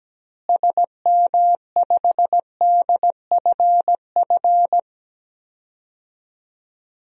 Morse code, heard as a keyed tone, about 17 wpm, 700 Hz. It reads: SM5DFF
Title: DXCC-100 Call Signs - 6 Characters - Repeated 2x - 17wpm